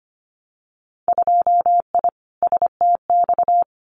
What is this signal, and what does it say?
Morse code, heard as a keyed tone, about 25 wpm, 700 Hz. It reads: 2I STX
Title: Winter Field Day Exchanges 25wpm